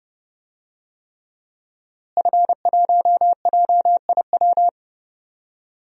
Morse code, heard as a keyed tone, about 30 wpm, 700 Hz. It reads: F1JIW
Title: DXCC-100 Call Signs - 5 Characters - Repeated 3x - 30wpm